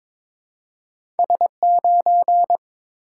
Morse code, heard as a keyed tone, about 22 wpm, 700 Hz. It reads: S9
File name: practice / 22wpm / Two Letter-Number Combinations - 22wpm